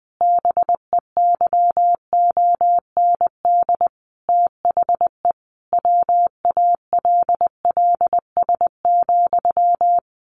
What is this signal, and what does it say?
Morse code, heard as a keyed tone, about 20 wpm, 700 Hz. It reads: BEYOND THE WALLS,
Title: A Sound Of Thunder 20wpm